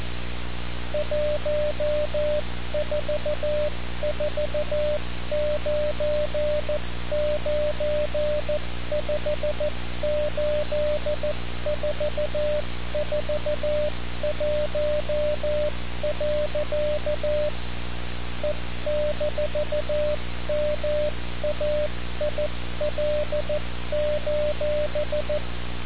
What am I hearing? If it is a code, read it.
Morse code, heard as a keyed tone, about 14 wpm, 600 Hz, with noise. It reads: 1449958441. E-MAIL:
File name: radiocap / REC z